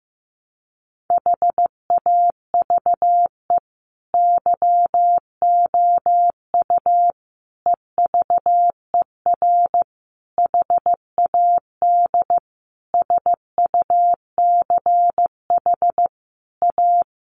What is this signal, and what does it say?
Morse code, heard as a keyed tone, about 15 wpm, 700 Hz. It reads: HAVE YOU EVER HAD SUCH A
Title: Sentences from Top 1200 Words 15wpm